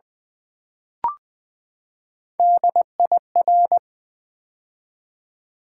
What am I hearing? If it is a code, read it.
Morse code, heard as a keyed tone, about 20 wpm, 700 Hz. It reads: DIR
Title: Build A Word - 3 Letters At A Time - 8-Letter Words - 20wpm